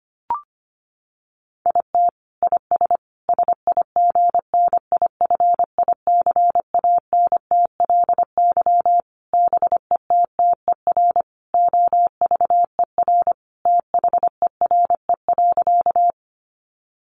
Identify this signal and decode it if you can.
Morse code, heard as a keyed tone, about 25 wpm, 700 Hz. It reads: IT IS SIGNIFICANTLY BETTER OVER THERE.
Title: Sentences from Top 1500 Words 25wpm